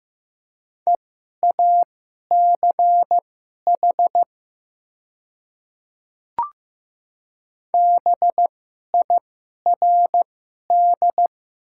Morse code, heard as a keyed tone, about 15 wpm, spaced as for 11 wpm, 700 Hz. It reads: EACH BIRD
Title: 2x Character Spacing - Top 500 Words - 15wpm